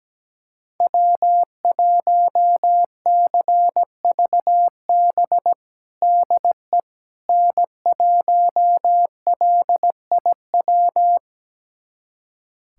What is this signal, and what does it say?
Morse code, heard as a keyed tone, about 17 wpm, 700 Hz. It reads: W1CVB DE N1LIW
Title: USA Callsign DE Callsign - 5 Characters - 17wpm